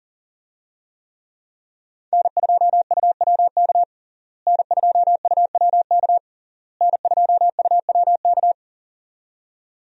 Morse code, heard as a keyed tone, about 40 wpm, 700 Hz. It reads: N2UWK N2UWK N2UWK
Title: USA Call Signs - 5 Characters - Repeated 3x - 40wpm